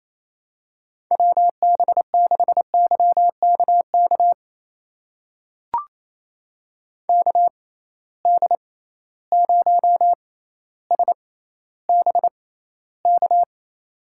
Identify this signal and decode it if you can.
Morse code, heard as a keyed tone, about 28 wpm, spaced as for 14 wpm, 700 Hz. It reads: WB6YKK KD0SBK
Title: USA Call Signs - 6 Characters - 6x Character Spacing - 28wpm